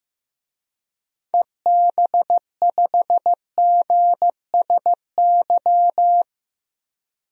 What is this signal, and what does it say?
Morse code, heard as a keyed tone, about 15 wpm, 700 Hz. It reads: EB5GSY
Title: DXCC-100 Call Signs - 6 Characters - 15wpm